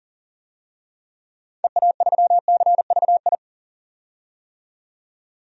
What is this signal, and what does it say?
Morse code, heard as a keyed tone, about 40 wpm, 700 Hz. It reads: EA3CVI
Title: DXCC-100 Call Signs - 6 Characters - 40wpm